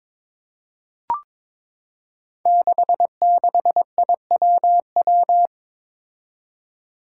Morse code, heard as a keyed tone, about 21 wpm, 700 Hz. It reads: 66IWW
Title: Five Challenging Letter-Number Combinations 22wpm